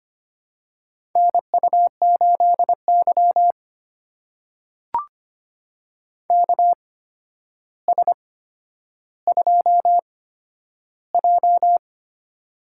Morse code, heard as a keyed tone, about 25 wpm, spaced as for 11 wpm, 700 Hz. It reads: NU8Y KS2J
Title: USA Call Signs - 4 Characters - 8x Character Spacing - 25wpm